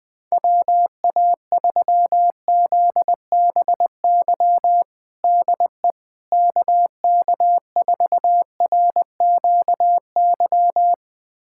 Morse code, heard as a keyed tone, about 20 wpm, 700 Hz. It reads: WA3ZBY DE KK4RQY
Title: USA Callsign DE Callsign 20wpm